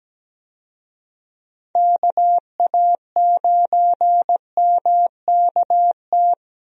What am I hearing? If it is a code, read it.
Morse code, heard as a keyed tone, about 17 wpm, 700 Hz. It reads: KA9MKT
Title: USA Call Signs - 6 Characters - Repeated 2x - 17wpm